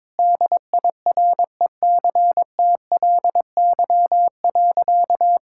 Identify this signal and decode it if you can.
Morse code, heard as a keyed tone, about 22 wpm, 700 Hz. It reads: DIRECTLY.
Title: Sentences from Top 1000 Words - Review Part 5 - 22wpm